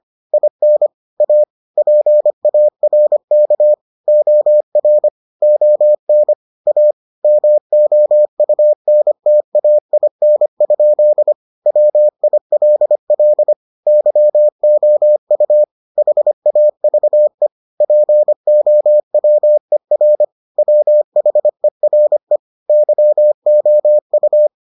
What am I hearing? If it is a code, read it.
Morse code, heard as a keyed tone, about 25 wpm, 600 Hz. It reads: IN A PARK OR ON A MOUNTAIN? WILL YOU HAVE POWER WHERE YOU